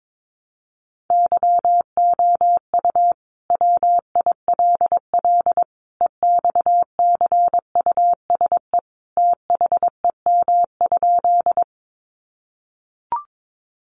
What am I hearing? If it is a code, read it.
Morse code, heard as a keyed tone, about 22 wpm, 700 Hz. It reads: YOU WILL EXCUSE THEM?
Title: Sentences from Top 700 Words 22wpm